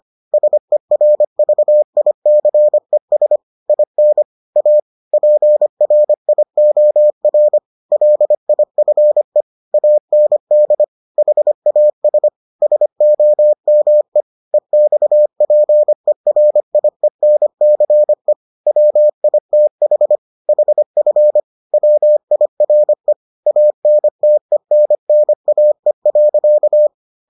Morse code, heard as a keyed tone, about 25 wpm, 600 Hz. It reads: SERVICES IN A PRIOR LIFE AND HAS SOME EXPERIENCE WITH HF WIRE ANTENNAE.